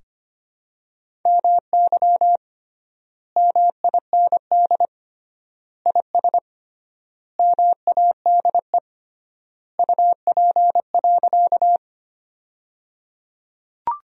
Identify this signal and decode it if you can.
Morse code, heard as a keyed tone, about 25 wpm, 700 Hz. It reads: MY MIND IS MADE UP.
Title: 3x Word-Spacing - Sentences from Top 500 Words - 25wpm